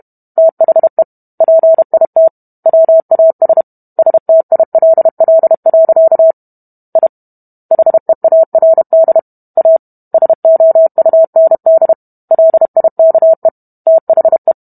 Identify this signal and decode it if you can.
Morse code, heard as a keyed tone, about 32 wpm, 650 Hz. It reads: THE PIT WAS STILL. I HEARD A SOUND LIKE THE